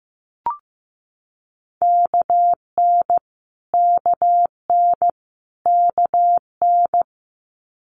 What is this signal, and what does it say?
Morse code, heard as a keyed tone, about 15 wpm, 700 Hz. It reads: KN KN KN
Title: Multi-letter Phonemes Repeated 3x 15wpm